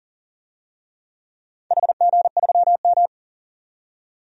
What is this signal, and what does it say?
Morse code, heard as a keyed tone, about 40 wpm, 700 Hz. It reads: HG3M
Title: DXCC-100 Call Signs - 4 Characters - 40wpm